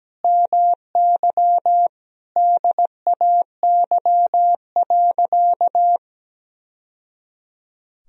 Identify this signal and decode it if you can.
Morse code, heard as a keyed tone, about 17 wpm, 700 Hz. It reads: MY DAY.
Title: Sentences from Top 1000 Words - Review Part 2 - 17wpm